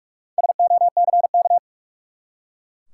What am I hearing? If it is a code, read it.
Morse code, heard as a keyed tone, about 45 wpm, 700 Hz. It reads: SOCK
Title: Four Letter Words - 45wpm